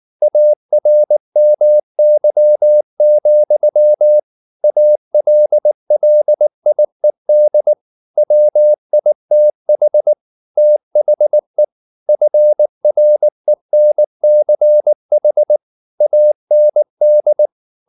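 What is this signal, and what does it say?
Morse code, heard as a keyed tone, about 19 wpm, 600 Hz. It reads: ARMY, ALLIED WITH THE FRENCH AND
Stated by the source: Morse code at 19 wpm from CW med Gnister, reading Ken Follett's novel Never in English.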